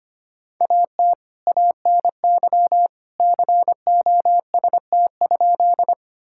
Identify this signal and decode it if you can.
Morse code, heard as a keyed tone, about 25 wpm, 700 Hz. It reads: AT ANY COST?
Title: Sentences from Top 1000 Words - Review Part 5 - 25wpm